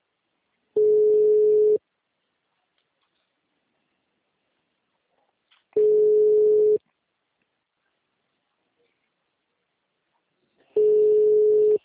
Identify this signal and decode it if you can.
Ringback tone (Europe, 425 Hz).